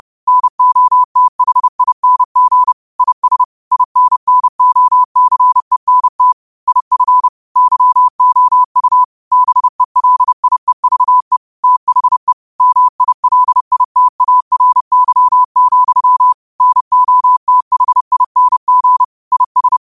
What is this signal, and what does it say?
Morse code, heard as a keyed tone, about 30 wpm, 1000 Hz. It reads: NOTHING IS INNOCENT IF YOU BELIEVE THE MILITARY, NOTHING IS
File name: qotdfiles / 30wpm